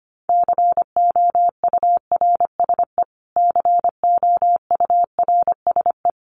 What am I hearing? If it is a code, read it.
Morse code, heard as a keyed tone, about 25 wpm, 700 Hz. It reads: COURSE COURSE
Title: Top 500 Words Repeated 25wpm